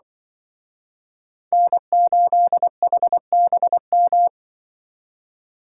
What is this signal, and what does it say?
Morse code, heard as a keyed tone, about 25 wpm, 700 Hz. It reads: N8HBM
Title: Speed Racing - USA 1x3 Call Signs 20wpm